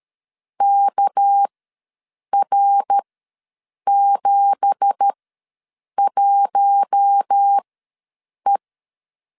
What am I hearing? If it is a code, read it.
Morse code, heard as a keyed tone, about 13 wpm, spaced as for 7 wpm, 800 Hz. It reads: KR71E
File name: LEZIONI / 9 2867 ws 12wpm 3431